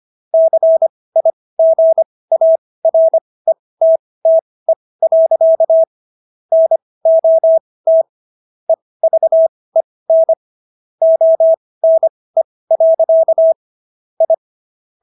Morse code, heard as a keyed tone, about 25 wpm, spaced as for 18 wpm, 650 Hz. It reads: CIGARETTE. NOT EVEN ONE. I